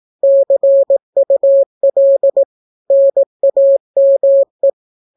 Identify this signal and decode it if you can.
Morse code, heard as a keyed tone, about 18 wpm, 550 Hz. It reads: CUL NAME
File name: naukapodstaw / skroty / skroty